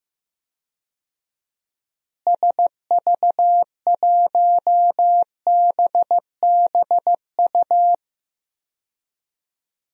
Morse code, heard as a keyed tone, about 15 wpm, 700 Hz. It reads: SV1BBU
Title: DXCC-100 Call Signs - 6 Characters - Repeated 2x - 15wpm